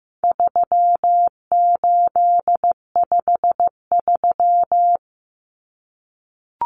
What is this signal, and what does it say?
Morse code, heard as a keyed tone, about 15 wpm, 700 Hz. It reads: 3853
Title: Common Cut Numbers - 4-Digit 15wpm